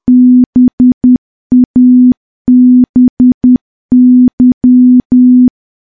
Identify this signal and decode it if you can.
Morse code, heard as a keyed tone, about 10 wpm, 250 Hz. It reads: BABY
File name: morse / morse-training-10WPM